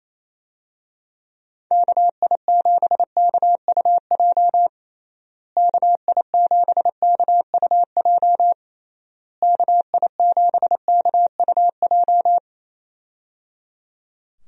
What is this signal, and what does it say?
Morse code, heard as a keyed tone, about 27 wpm, 700 Hz. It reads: KI7KUJ KI7KUJ KI7KUJ
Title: USA Call Signs - 6 Characters - Repeated 3x - 28wpm